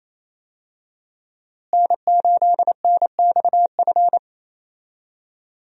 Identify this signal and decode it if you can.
Morse code, heard as a keyed tone, about 28 wpm, 700 Hz. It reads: N8NXF
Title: USA Call Signs - 5 Characters - Repeated 2x - 28wpm